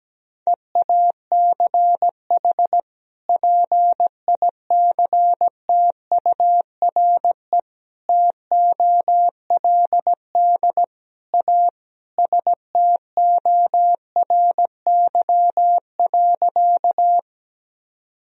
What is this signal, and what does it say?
Morse code, heard as a keyed tone, about 17 wpm, 700 Hz. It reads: EACH PICTURE TOLD A STORY.
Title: Sentences from Top 1000 Words - Review Part 1 - 17wpm